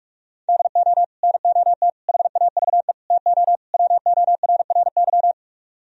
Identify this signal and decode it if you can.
Morse code, heard as a keyed tone, about 50 wpm, 700 Hz. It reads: DO NOT HAVE TO WORRY